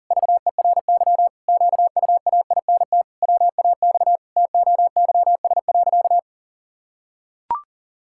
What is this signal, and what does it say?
Morse code, heard as a keyed tone, about 37 wpm, 700 Hz. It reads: VERY QUAINT WAX TOYS.
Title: Pangram Challenge v2 - 40wpm